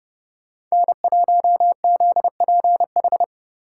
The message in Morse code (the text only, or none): N1ZPH